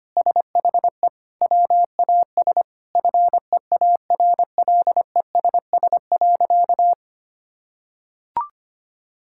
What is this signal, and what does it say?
Morse code, heard as a keyed tone, about 25 wpm, 700 Hz. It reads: SHE WAS FEARLESS.
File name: practice / 25wpm / Sentences with 4-Word Phrases - 25wpm